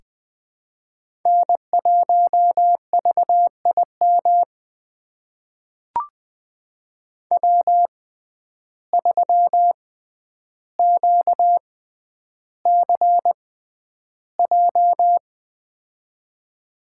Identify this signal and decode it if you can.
Morse code, heard as a keyed tone, about 20 wpm, spaced as for 10 wpm, 700 Hz. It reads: N1VIM W3QCJ
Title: USA Call Signs - 5 Characters - 6x Character Spacing - 20wpm